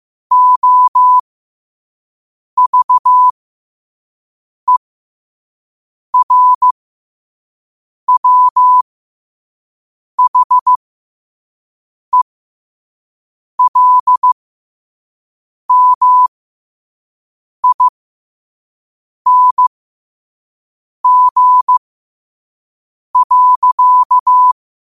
Morse code, heard as a keyed tone, about 15 wpm, spaced as for 5 wpm, 1000 Hz. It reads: OVERWHELMING.
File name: qotdfiles / 5wpm